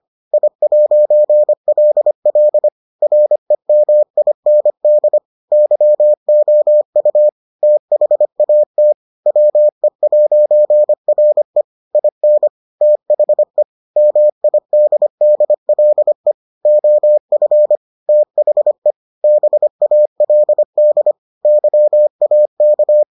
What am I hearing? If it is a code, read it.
Morse code, heard as a keyed tone, about 25 wpm, 600 Hz. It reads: I'LL REMIND YOU THAT WE'RE IN THE MIDDLE OF THE BALD YAK